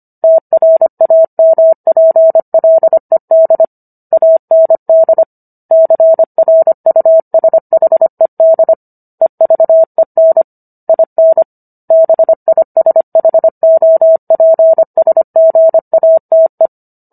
Morse code, heard as a keyed tone, about 25 wpm, 650 Hz. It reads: TRAMPLED AND CRUSHED EVEN IN BISHOPSGATE